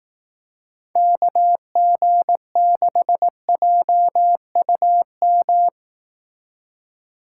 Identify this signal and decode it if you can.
Morse code, heard as a keyed tone, about 18 wpm, 700 Hz. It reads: KG6JUM
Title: Speed Racing - USA 2x3 Call Signs 15wpm